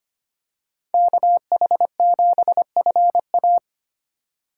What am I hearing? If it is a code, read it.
Morse code, heard as a keyed tone, about 25 wpm, 700 Hz. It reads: KH7FA
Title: USA Call Signs - 5 Characters - 2x Character Spacing - 25wpm